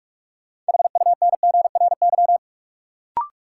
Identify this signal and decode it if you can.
Morse code, heard as a keyed tone, about 46 wpm, 700 Hz. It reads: HUNGRY
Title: Sentences from Top 1400 Words - New Word Drill Alt - 45wpm